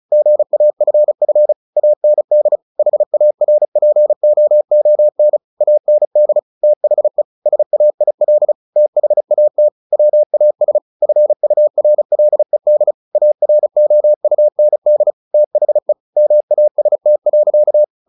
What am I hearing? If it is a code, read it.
Morse code, heard as a keyed tone, about 34 wpm, 600 Hz. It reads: GAFF AND HARPOON AND THE SAIL THAT WAS FURLED AROUND THE MAST.
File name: the old man and the sea / the old man and the sea 1 35wpm